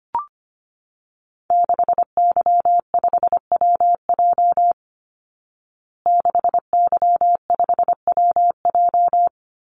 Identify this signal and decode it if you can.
Morse code, heard as a keyed tone, about 25 wpm, 700 Hz. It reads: 6Y5WJ 6Y5WJ
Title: Call Signs Repeated 3x - 25wpm